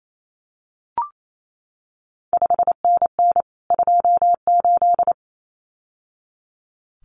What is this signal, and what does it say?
Morse code, heard as a keyed tone, about 28 wpm, 700 Hz. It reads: 5NN 28
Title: CQ WW DX Contest Exchange Encore - 28wpm